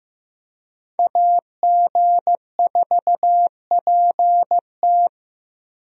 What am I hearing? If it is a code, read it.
Morse code, heard as a keyed tone, about 16 wpm, 700 Hz. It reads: AG4PT
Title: USA Call Signs - 5 Characters - Repeated 2x - 15wpm